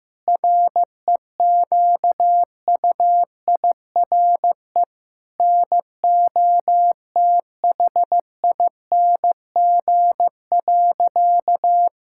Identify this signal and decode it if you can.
Morse code, heard as a keyed tone, about 16 wpm, 700 Hz. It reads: REQUIRE NOTHING.
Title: Sentences from Top 1000 Words - Review Part 6 - 15wpm